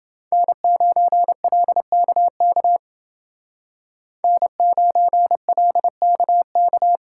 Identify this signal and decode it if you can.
Morse code, heard as a keyed tone, about 29 wpm, 700 Hz. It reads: N9LKK N9LKK